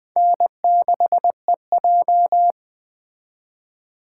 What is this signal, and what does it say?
Morse code, heard as a keyed tone, about 20 wpm, 700 Hz. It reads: N6EJ